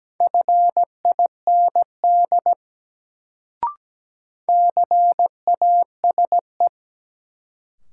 Morse code, heard as a keyed tone, about 17 wpm, 700 Hz. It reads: FIND CASE
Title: Sets of 2 Words - New Word Drill - 17wpm